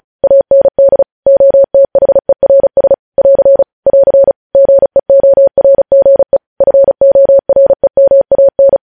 Morse code, heard as a keyed tone, about 36 wpm, 550 Hz. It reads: AND OTHERS <AR> <AR> GEORGE FOREMAN
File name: podcast / News headlines-1x-35x35